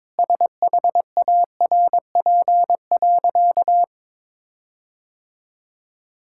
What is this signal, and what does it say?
Morse code, heard as a keyed tone, about 22 wpm, 700 Hz. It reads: SHARP.
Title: Sentences from Top 1600 Words 22wpm